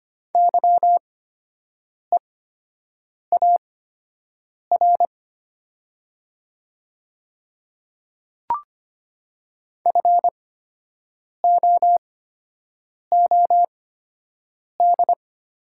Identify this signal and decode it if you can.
Morse code, heard as a keyed tone, about 25 wpm, spaced as for 7 wpm, 700 Hz. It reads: YEAR FOOD
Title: 8x Character Spacing - Top 300 Words - 25wpm